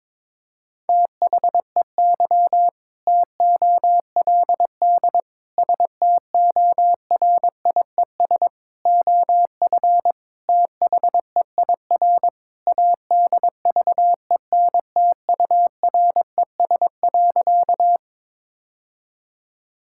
Morse code, heard as a keyed tone, about 22 wpm, 700 Hz. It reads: THEY TOLD STORIES OF THEIR ADVENTURES.
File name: practice / 22wpm / Sentences with 2-Word Phrases - 22wpm